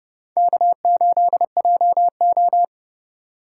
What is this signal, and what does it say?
Morse code, heard as a keyed tone, about 30 wpm, 700 Hz. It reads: K8JO